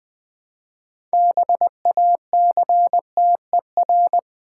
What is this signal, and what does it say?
Morse code, heard as a keyed tone, about 20 wpm, 700 Hz. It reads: BACTER
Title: Build A Word - 3 Letters At A Time - 8-Letter Words - 20wpm